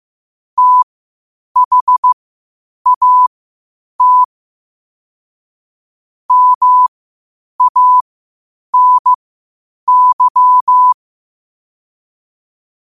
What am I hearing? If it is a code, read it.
Morse code, heard as a keyed tone, about 15 wpm, spaced as for 8 wpm, 1000 Hz. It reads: THAT MANY
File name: qotdfiles / 8wpm